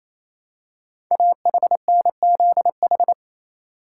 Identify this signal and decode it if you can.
Morse code, heard as a keyed tone, about 27 wpm, 700 Hz. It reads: AHNZH
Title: Code Groups - 5 Letters - 28wpm